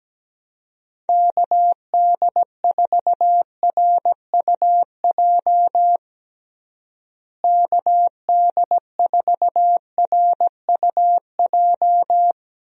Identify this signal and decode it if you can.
Morse code, heard as a keyed tone, about 17 wpm, 700 Hz. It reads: KD4RUJ KD4RUJ
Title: USA Call Signs - 6 Characters - Repeated 2x - 17wpm